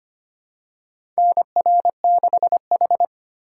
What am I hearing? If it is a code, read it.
Morse code, heard as a keyed tone, about 25 wpm, 700 Hz. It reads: NR6H
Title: USA Call Signs - 4 Characters - 2x Character Spacing - 25wpm